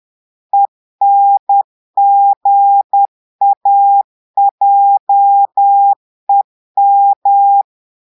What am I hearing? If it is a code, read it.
Morse code, heard as a keyed tone, about 10 wpm, 800 Hz. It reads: ENGAJEM
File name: portugais texte en clair 2 vitesse 10 mots minutes